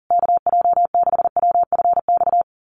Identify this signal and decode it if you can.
Morse code, heard as a keyed tone, about 40 wpm, 700 Hz. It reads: KJ6WFX
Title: USA Call Signs Repeated 3x 40wpm